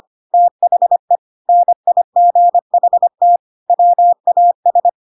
Morse code, heard as a keyed tone, about 25 wpm, 700 Hz. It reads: THE NIGHT WAS
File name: Pinocchio / Pinocchio ch6 25 25 ik1hge